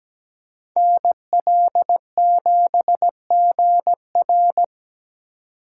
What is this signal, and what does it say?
Morse code, heard as a keyed tone, about 20 wpm, 700 Hz. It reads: NL7GR
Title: Speed Racing - USA 2x2 Call Signs 17wpm